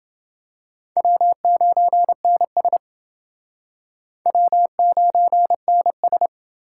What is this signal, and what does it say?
Morse code, heard as a keyed tone, about 28 wpm, 700 Hz. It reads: W9NS W9NS